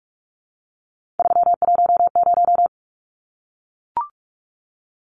Morse code, heard as a keyed tone, about 44 wpm, 700 Hz. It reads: 310
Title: Numbers - 3-Digit 45wpm